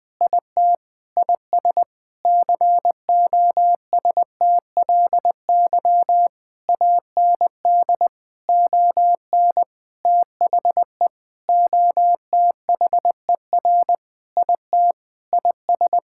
Morse code, heard as a keyed tone, about 20 wpm, 700 Hz. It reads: IT IS COSTLY AND ON THE OTHER IT IS
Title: Sentences with 8-Word Phrases - Spoken Beginning - 20wpm